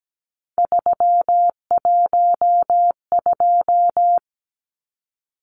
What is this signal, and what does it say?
Morse code, heard as a keyed tone, about 17 wpm, 700 Hz. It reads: 312
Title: Numbers - 3-Digit 17wpm